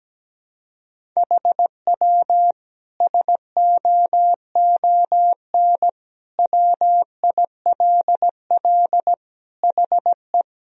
Morse code, heard as a keyed tone, about 17 wpm, 700 Hz. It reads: HW SOON WILL HE
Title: Sentences from Top 400 Words and Abbreviations 17wpm